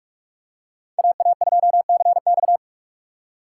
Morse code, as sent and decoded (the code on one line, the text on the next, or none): .- .- ..--- -.-. -..-
AA2CX